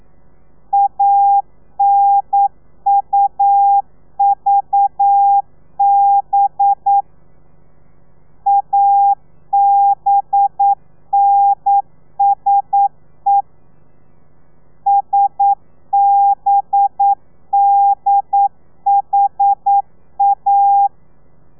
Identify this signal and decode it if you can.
Morse code, heard as a keyed tone, about 9 wpm, 800 Hz. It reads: ANUVB ABNSE SBDHA
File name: telegrafia / i2tfj / Lez4 40